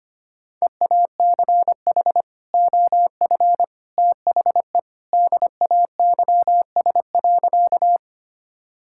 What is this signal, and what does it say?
Morse code, heard as a keyed tone, about 25 wpm, 700 Hz. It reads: EACH OF THE DAYS.
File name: practice / 25wpm / Sentences with 3-Word Phrases - 25wpm